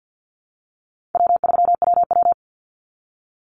Morse code, heard as a keyed tone, about 49 wpm, 700 Hz. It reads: W3WW
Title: Call Signs Repeated 3x - 50wpm